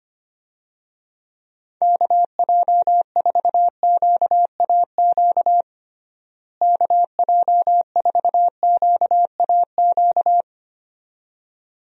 Morse code, heard as a keyed tone, about 25 wpm, 700 Hz. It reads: KJ4QAQ KJ4QAQ
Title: USA Call Signs - 6 Characters - Repeated 2x - 25wpm